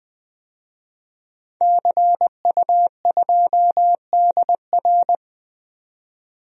Morse code, heard as a keyed tone, about 20 wpm, 700 Hz. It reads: CU2DR
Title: DXCC-100 Call Signs - 5 Characters - 20wpm